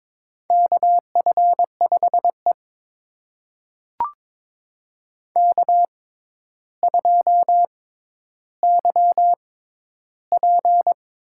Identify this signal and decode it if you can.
Morse code, heard as a keyed tone, about 22 wpm, spaced as for 11 wpm, 700 Hz. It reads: KF5E K2YP
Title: USA Call Signs - 4 Characters - 6x Character Spacing - 22wpm